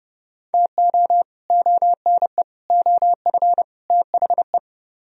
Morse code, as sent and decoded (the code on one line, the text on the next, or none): - --- / --- -. / --- ..-. / - .... .
TO ONE OF THE